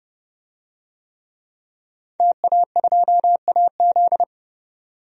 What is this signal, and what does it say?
Morse code, heard as a keyed tone, about 30 wpm, 700 Hz. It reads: TA2AZ
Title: DXCC-100 Call Signs - 5 Characters - 30wpm